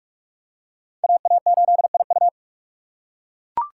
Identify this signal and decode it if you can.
Morse code, heard as a keyed tone, about 41 wpm, 700 Hz. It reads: AA8IU
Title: USA Call Signs - 6x Character Spacing - 45wpm